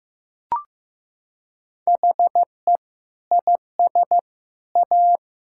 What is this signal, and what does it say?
Morse code, heard as a keyed tone, about 15 wpm, 700 Hz. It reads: HE IS A
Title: Sentences from Top 1000 Words - Review Part 6 - 15wpm